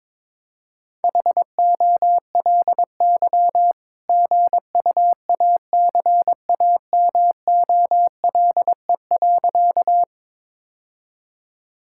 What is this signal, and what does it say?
Morse code, heard as a keyed tone, about 22 wpm, 700 Hz. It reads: HOLY GUACAMOLE.
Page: morse-code-ninja-practice-22wpm